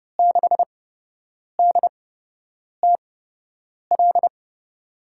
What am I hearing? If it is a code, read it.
Morse code, heard as a keyed tone, about 30 wpm, spaced as for 8 wpm, 700 Hz. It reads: 6DTL
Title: USA Call Signs - 6 Characters - 8x Character Spacing - 30wpm